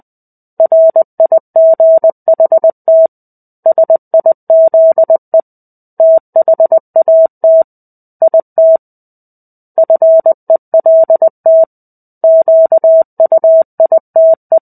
Morse code, heard as a keyed tone, about 20 wpm, 650 Hz. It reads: RIGHT SIZE THAT IT FELT QUITE
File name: 05-Chapter-05 20 WPM